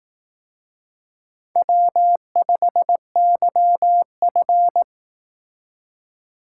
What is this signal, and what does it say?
Morse code, heard as a keyed tone, about 18 wpm, 700 Hz. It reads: W5YF
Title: Speed Racing - USA 1x2 Call Signs 15wpm